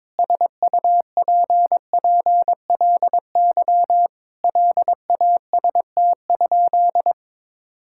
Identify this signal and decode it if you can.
Morse code, heard as a keyed tone, about 22 wpm, 700 Hz. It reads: SUPPLY LAST?
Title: Sentences from Top 1000 Words - Review Part 5 - 22wpm